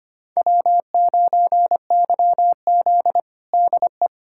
Morse code, heard as a keyed tone, about 25 wpm, 700 Hz. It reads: W9YZ DE
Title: USA Callsign DE Callsign 25wpm